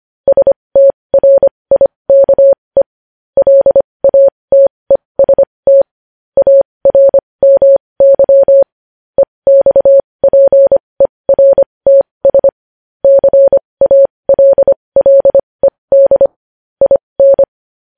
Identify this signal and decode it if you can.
Morse code, heard as a keyed tone, about 25 wpm, spaced as for 20 wpm, 550 Hz. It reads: STRIKE LATEST ARMY EXPERTS CALLED IN
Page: news-headlines-in-morse-code-at-20-wpm